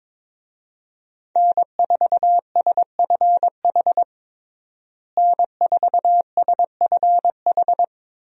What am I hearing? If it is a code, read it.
Morse code, heard as a keyed tone, about 22 wpm, 700 Hz. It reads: N4SFH N4SFH
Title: USA Call Signs - 5 Characters - Repeated 2x - 22wpm